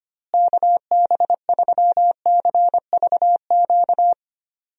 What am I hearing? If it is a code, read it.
Morse code, heard as a keyed tone, about 25 wpm, 700 Hz. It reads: KB3CVQ
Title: USA Call Signs - 6 Characters - Repeated 3x - 25wpm